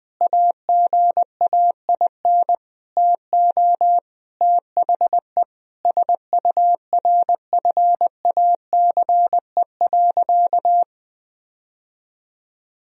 Morse code, heard as a keyed tone, about 20 wpm, 700 Hz. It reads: AGAIN TO THE SURFACE.
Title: Sentences from Top 1000 Words - Review Part 3 - 20wpm